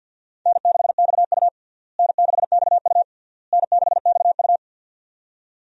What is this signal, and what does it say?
Morse code, heard as a keyed tone, about 53 wpm, 700 Hz. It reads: N6XU N6XU N6XU
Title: USA Call Signs - 4 Characters - Repeated 3x - 50wpm